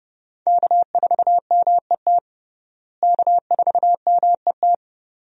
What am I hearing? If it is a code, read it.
Morse code, heard as a keyed tone, about 32 wpm, 700 Hz. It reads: K4MET K4MET
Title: USA Call Signs - 5 Characters - Repeated 2x - 30wpm